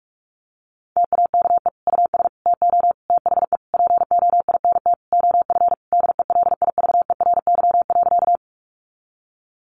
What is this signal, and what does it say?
Morse code, heard as a keyed tone, about 44 wpm, 700 Hz. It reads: TAKE US TO THE POINT OF DELIVERY.
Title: Sentences from Top 1500 Words 45wpm